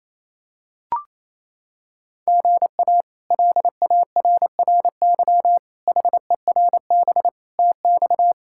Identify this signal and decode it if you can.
Morse code, heard as a keyed tone, about 28 wpm, 700 Hz. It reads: GA LARRY HERB TX
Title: K1USN SST QSO Practice - 28wpm